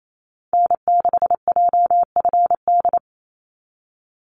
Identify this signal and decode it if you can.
Morse code, heard as a keyed tone, about 28 wpm, 700 Hz. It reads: N6JFD
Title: NA SOTA Activator Callsigns 28wpm